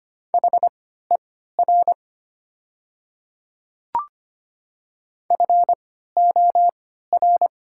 Morse code, heard as a keyed tone, about 25 wpm, spaced as for 14 wpm, 690 Hz. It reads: HER FOR
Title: 3x Character Spacing - Top 100 Words - 25wpm